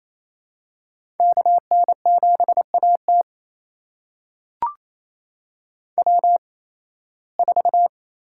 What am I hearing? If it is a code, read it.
Morse code, heard as a keyed tone, about 28 wpm, spaced as for 18 wpm, 700 Hz. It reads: KN7AT W4
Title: USA Call Signs - 5 Characters - 8x Character Spacing - 28wpm